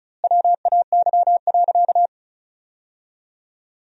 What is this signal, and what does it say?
Morse code, heard as a keyed tone, about 34 wpm, 700 Hz. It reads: WAY.
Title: 2x Word-Spacing - Sentences from Top 300 Words - 35wpm